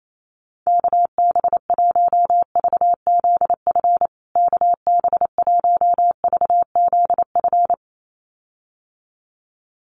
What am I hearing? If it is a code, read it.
Morse code, heard as a keyed tone, about 28 wpm, 700 Hz. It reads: KB1VZF KB1VZF
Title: USA Call Signs Repeated 2x 28wpm